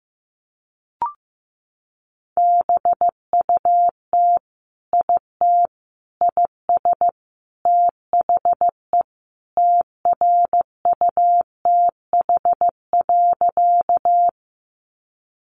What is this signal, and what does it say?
Morse code, heard as a keyed tone, about 15 wpm, 700 Hz. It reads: BUT IT IS THE TRUTH.
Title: Sentences from Top 800 Words 15wpm